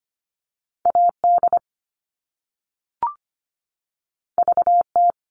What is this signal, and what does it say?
Morse code, heard as a keyed tone, about 25 wpm, 700 Hz. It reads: AD VT
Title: All Cut Numbers - 2-Digit 25wpm